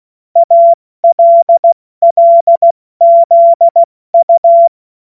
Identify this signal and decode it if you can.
Morse code, heard as a keyed tone, about 16 wpm, 680 Hz. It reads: ALLZU